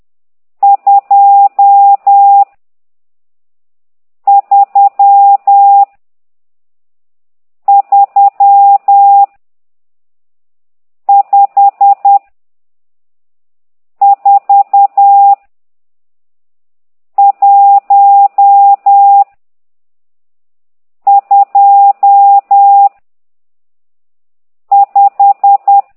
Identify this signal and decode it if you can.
Morse code, heard as a keyed tone, about 10 wpm, 800 Hz. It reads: 2 3 3 5 4 1 2 5
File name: ct2idw / exerc12